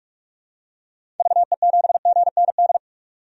Morse code, heard as a keyed tone, about 46 wpm, 700 Hz. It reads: VE7GND
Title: DXCC-100 Call Signs - 6 Characters - 45wpm